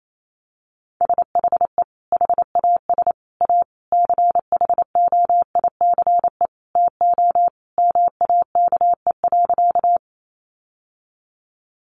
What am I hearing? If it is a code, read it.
Morse code, heard as a keyed tone, about 28 wpm, 700 Hz. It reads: SHE HAS A CHOICE TO MAKE.
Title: Sentences from Top 900 Words 28wpm